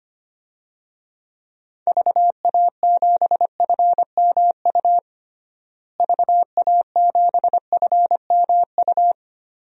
Morse code, heard as a keyed tone, about 25 wpm, 700 Hz. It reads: VA7FMU VA7FMU
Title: DXCC-100 Call Signs - 6 Characters - Repeated 2x - 25wpm